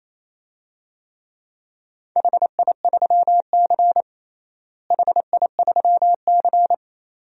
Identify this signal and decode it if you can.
Morse code, heard as a keyed tone, about 28 wpm, 700 Hz. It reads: HI3C HI3C
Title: DXCC-100 Call Signs - 4 Characters - Repeated 2x - 28wpm